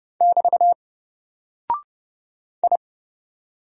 Morse code, heard as a keyed tone, about 30 wpm, 700 Hz. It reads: <BT> I
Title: Single Character (Letter:Number:Punctuation:Prosign) 30wpm